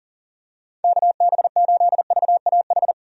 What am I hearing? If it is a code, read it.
Morse code, heard as a keyed tone, about 40 wpm, 700 Hz. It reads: KB8VAH
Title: USA Call Signs - 6 Characters - 40wpm